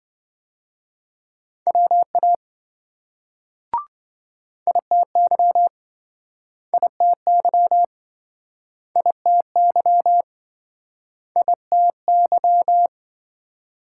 Morse code, heard as a keyed tone, about 28 wpm, 700 Hz. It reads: WA ITY ITY ITY ITY